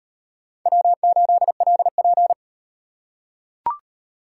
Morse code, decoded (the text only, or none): W8LP